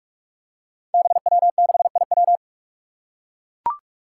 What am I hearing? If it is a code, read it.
Morse code, heard as a keyed tone, about 42 wpm, 700 Hz. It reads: BW6IW
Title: Five Challenging Letter-Number Combinations 45wpm